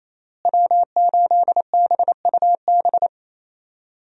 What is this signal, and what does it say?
Morse code, heard as a keyed tone, about 27 wpm, 700 Hz. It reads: W8BUB
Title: USA Call Signs - 5 Characters - Repeated 2x - 28wpm